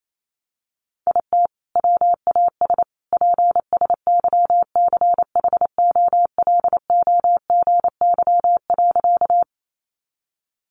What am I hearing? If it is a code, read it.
Morse code, heard as a keyed tone, about 28 wpm, 700 Hz. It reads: IT WAS PSYCHOLOGY.
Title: Sentences from Top 1400 Words 28wpm